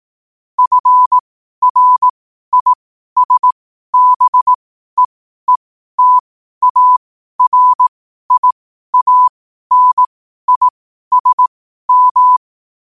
Morse code, heard as a keyed tone, about 18 wpm, spaced as for 12 wpm, 1000 Hz. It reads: FRISBEETARIANISM
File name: qotdfiles / 12wpm